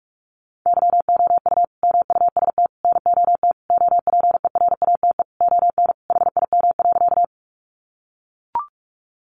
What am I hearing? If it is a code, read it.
Morse code, heard as a keyed tone, about 44 wpm, 700 Hz. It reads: YOU MUST NOT OPERATE ON HIM.
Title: Sentences from Top 1500 Words 45wpm